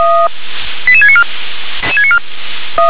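Five-tone selective call (ZVEI), heard as quick runs of five tones, tones of about 70 ms. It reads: 88694
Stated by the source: ZVEI five-tone selective call received on SDR.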